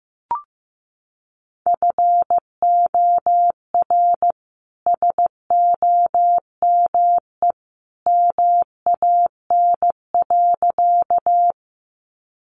Morse code, heard as a keyed tone, about 15 wpm, 700 Hz. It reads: FOR SOME MAN.
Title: Sentences from Top 200 Words 15wpm